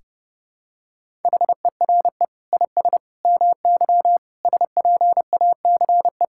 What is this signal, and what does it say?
Morse code, heard as a keyed tone, about 29 wpm, 700 Hz. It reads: HERE IS MY SPACE
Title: Sentences from Top 1000 Words - Review Part 5 - 30wpm